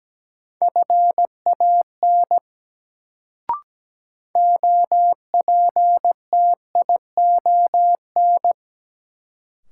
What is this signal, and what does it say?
Morse code, heard as a keyed tone, about 17 wpm, 700 Hz. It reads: FAN OPTION
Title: Sentences from Top 1300 Words - New Word Drill Alt - 17wpm